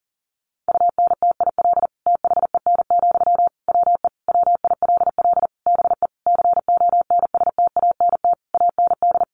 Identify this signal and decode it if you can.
Morse code, heard as a keyed tone, about 40 wpm, 700 Hz. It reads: UNTIL THEN, WE WILL BE CONSTANT AND
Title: Sentences from Top 1500 Words 40wpm